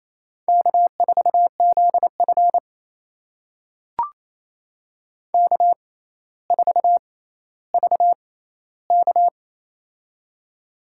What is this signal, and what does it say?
Morse code, heard as a keyed tone, about 28 wpm, spaced as for 14 wpm, 700 Hz. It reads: K4ZF K4VK
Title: USA Call Signs - 4 Characters - 6x Character Spacing - 28wpm